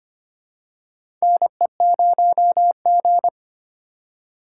Morse code, heard as a keyed tone, about 25 wpm, 700 Hz. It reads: NE0G
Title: USA Call Signs - 4 Characters - 4x Character Spacing - 25wpm